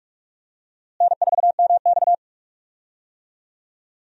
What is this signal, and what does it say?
Morse code, heard as a keyed tone, about 46 wpm, 700 Hz. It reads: N4MX
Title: USA Call Signs - 4 Characters - Repeated 3x - 45wpm